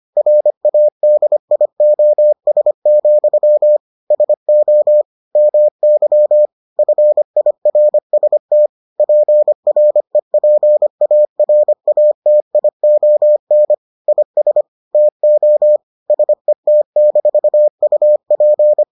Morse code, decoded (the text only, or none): RADIOS, SO MY FIRST PREPARATION IS TO SET-UP